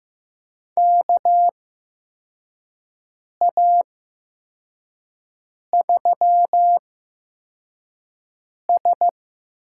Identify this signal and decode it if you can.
Morse code, heard as a keyed tone, about 15 wpm, 700 Hz. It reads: K A 3 S
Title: USA Call Signs - 8x Character Spacing - 15wpm